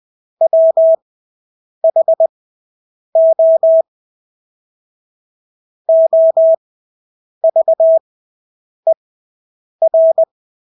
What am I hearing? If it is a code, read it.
Morse code, heard as a keyed tone, about 20 wpm, spaced as for 8 wpm, 650 Hz. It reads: WHO OVER
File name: Engelske ord 20/8 wpm